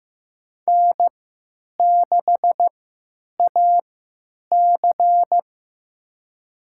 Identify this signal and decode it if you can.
Morse code, heard as a keyed tone, about 15 wpm, spaced as for 9 wpm, 700 Hz. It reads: N6AC
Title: USA Call Signs - 4 Characters - 3x Character Spacing - 15wpm